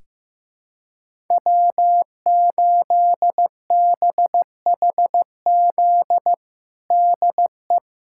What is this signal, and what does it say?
Morse code, heard as a keyed tone, about 15 wpm, 700 Hz. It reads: W8BHZ DE
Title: USA Callsign DE Callsign - 5 Characters - 15wpm